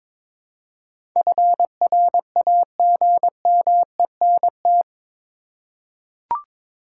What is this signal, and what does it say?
Morse code, heard as a keyed tone, about 21 wpm, 700 Hz. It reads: FRAGMENT
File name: practice / 22wpm / Eight Letter Words - 22wpm